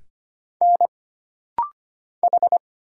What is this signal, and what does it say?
Morse code, heard as a keyed tone, about 25 wpm, 700 Hz. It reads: N H
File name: Single Character (Letter:Number:Punctuation:Prosign) - Rapid-Fire - 25wpm